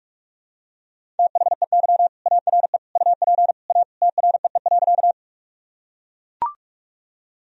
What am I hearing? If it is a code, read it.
Morse code, heard as a keyed tone, about 42 wpm, 700 Hz. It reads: THEY ARE UP A TREE.
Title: Sentences from Top 1000 Words - Review Part 1 - 45wpm